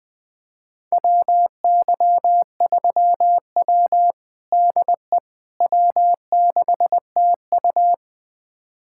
Morse code, heard as a keyed tone, about 20 wpm, 700 Hz. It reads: WY3W DE W6TU
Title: USA Callsign DE Callsign - 4 Characters - 20wpm